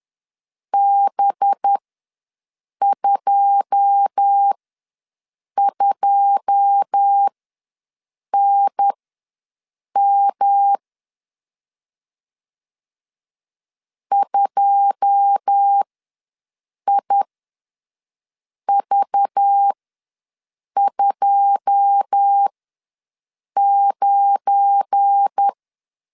Morse code, heard as a keyed tone, about 10 wpm, spaced as for 6 wpm, 800 Hz. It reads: B22NM 2IV29
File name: LEZIONI / 9 2 ws 10wpm 4130